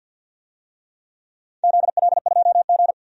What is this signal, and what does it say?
Morse code, heard as a keyed tone, about 53 wpm, 700 Hz. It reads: ZL2G
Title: DXCC-100 Call Signs - 4 Characters - 50wpm